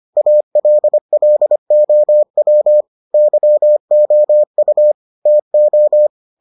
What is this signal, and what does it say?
Morse code, heard as a keyed tone, about 25 wpm, 600 Hz. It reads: ALLOW YOU TO